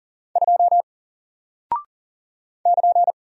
Morse code, heard as a keyed tone, about 38 wpm, 700 Hz. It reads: 2 <KN>
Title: Single Character (Letter:Number:Punctuation:Prosign) 40wpm